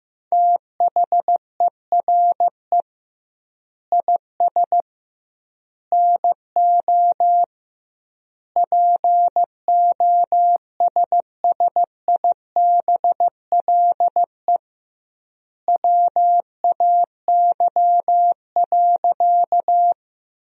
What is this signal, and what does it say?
Morse code, heard as a keyed tone, about 16 wpm, 700 Hz. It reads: THERE IS NO POSSIBLE WAY.
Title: 2x Word-Spacing - Sentences from Top 500 Words - 15wpm